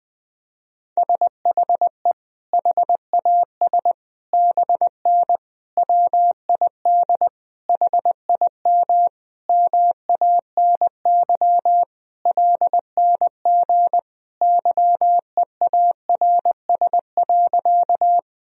Morse code, heard as a keyed tone, about 20 wpm, 700 Hz. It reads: SHE HAS BN WID HIM MANY LNG YEARS.
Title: Sentences from Top 200 Words and Abbreviations 20wpm